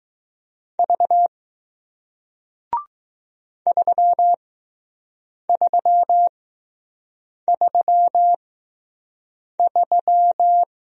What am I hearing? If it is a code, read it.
Morse code, heard as a keyed tone, about 20 wpm, spaced as for 6 wpm, 700 Hz. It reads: V 3333